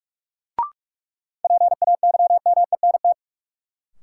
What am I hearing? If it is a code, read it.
Morse code, heard as a keyed tone, about 46 wpm, 700 Hz. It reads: PAYMENT
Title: Sentences from Top 1200 Words - New Word Drill - 45wpm